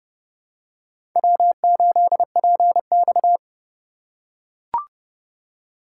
Morse code, heard as a keyed tone, about 29 wpm, 700 Hz. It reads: W8PX
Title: USA Call Signs - 4 Characters - 3x Character Spacing - 30wpm